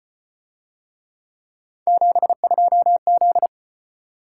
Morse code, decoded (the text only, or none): 72Z